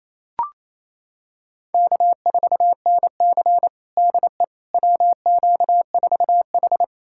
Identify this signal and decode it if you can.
Morse code, heard as a keyed tone, about 28 wpm, 700 Hz. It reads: K4NC DE WQ4H
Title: USA Callsign DE Callsign - 4 Characters - 28wpm